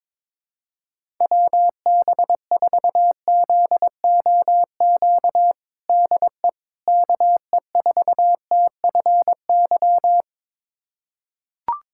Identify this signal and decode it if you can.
Morse code, heard as a keyed tone, about 22 wpm, 700 Hz. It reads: WB4ZOQ DE KE4TFY
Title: USA Callsign DE Callsign 22wpm